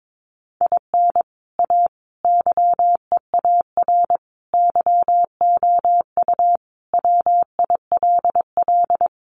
Morse code, heard as a keyed tone, about 22 wpm, 700 Hz. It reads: IN A YEAR YOU WILL
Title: Sentences from Top 600 Words 22wpm